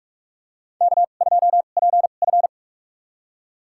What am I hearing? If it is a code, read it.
Morse code, heard as a keyed tone, about 46 wpm, spaced as for 33 wpm, 700 Hz. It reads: K2PF
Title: USA Call Signs - 2x Character Spacing - 45wpm